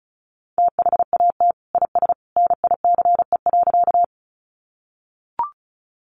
Morse code, heard as a keyed tone, about 34 wpm, 700 Hz. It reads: THAT IS NICE.
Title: Sentences from Top 1100 Words 35wpm